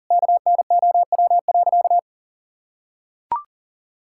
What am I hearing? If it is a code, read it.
Morse code, heard as a keyed tone, about 40 wpm, 700 Hz. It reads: KNOW.
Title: Sentences from Top 1000 Words - Review Part 5 - 40wpm